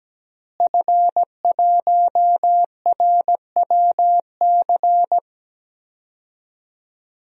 Morse code, heard as a keyed tone, about 17 wpm, 700 Hz. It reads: F1RWC
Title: DXCC-100 Call Signs - 5 Characters - Repeated 3x - 17wpm